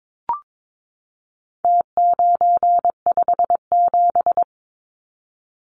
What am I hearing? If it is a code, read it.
Morse code, heard as a keyed tone, about 22 wpm, 700 Hz. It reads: T957
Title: Numbers - Contest Style - 4-Digit 22wpm